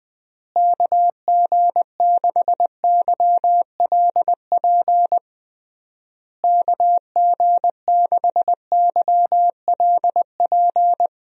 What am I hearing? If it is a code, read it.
Morse code, heard as a keyed tone, about 20 wpm, 700 Hz. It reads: KG6YLP KG6YLP
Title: USA Call Signs - 6 Characters - Repeated 3x - 20wpm